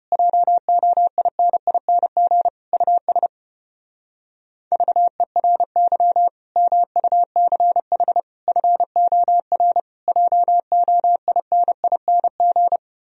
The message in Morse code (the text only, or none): JOINING US VERY MUCH FOR JOINING